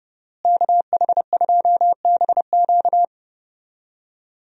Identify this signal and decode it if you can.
Morse code, heard as a keyed tone, about 30 wpm, 700 Hz. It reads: KH2BQ
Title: USA Call Signs - 5 Characters - Repeated 2x - 30wpm